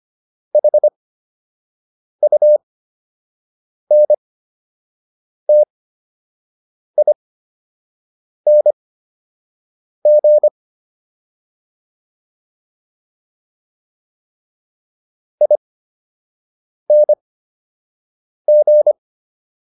Morse code, heard as a keyed tone, about 25 wpm, spaced as for 6 wpm, 600 Hz. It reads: HUNTING ING